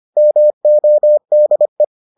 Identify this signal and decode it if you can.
Morse code, heard as a keyed tone, about 25 wpm, 600 Hz. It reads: MODE